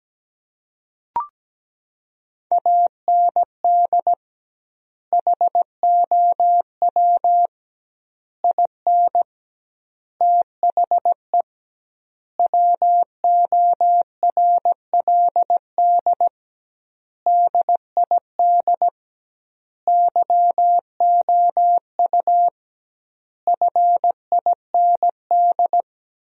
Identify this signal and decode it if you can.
Morse code, heard as a keyed tone, about 17 wpm, 700 Hz. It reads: AND HOW IN THE WORLD DID YOU FIND
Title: 2x Word-Spacing - Sentences from Top 200 Words - 17wpm